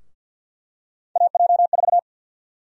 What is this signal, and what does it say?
Morse code, heard as a keyed tone, about 52 wpm, 700 Hz. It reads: AJ4
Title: USA Call Sign Prefixes - 50wpm